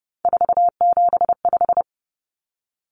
Morse code, heard as a keyed tone, about 30 wpm, 700 Hz. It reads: 475
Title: Common Cut Numbers - 3-Digit 30wpm